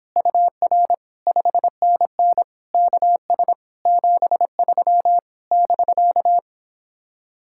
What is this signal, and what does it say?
Morse code, heard as a keyed tone, about 29 wpm, 700 Hz. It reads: UR 5NN KS 73 <BK>